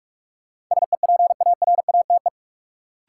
Morse code, heard as a keyed tone, about 45 wpm, 700 Hz. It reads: SEPARATE
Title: Sentences from Top 1000 Words - Review Part 7 - New Word Drill - 45wpm